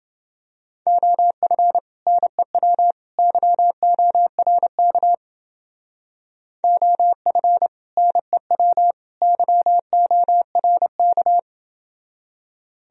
Morse code, heard as a keyed tone, about 29 wpm, 700 Hz. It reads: OF NEW YORK OF NEW YORK